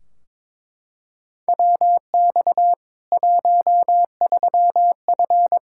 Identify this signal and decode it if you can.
Morse code, heard as a keyed tone, about 22 wpm, 700 Hz. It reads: WX 13F
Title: K1USN SST QSO Practice 22wpm